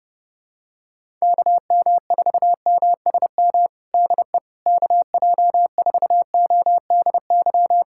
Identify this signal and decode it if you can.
Morse code, heard as a keyed tone, about 30 wpm, 700 Hz. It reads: KM4MSM DE KJ4ODY
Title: USA Callsign DE Callsign 30wpm